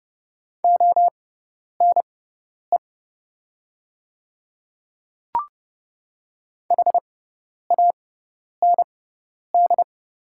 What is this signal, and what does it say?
Morse code, heard as a keyed tone, about 30 wpm, spaced as for 10 wpm, 690 Hz. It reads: ONE HAND
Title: 6x Character Spacing - Top 200 Words - 30wpm